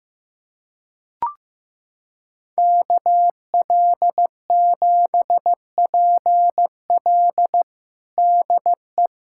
Morse code, heard as a keyed tone, about 15 wpm, 700 Hz. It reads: KL7PL DE
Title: USA Callsign DE Callsign - 5 Characters - 15wpm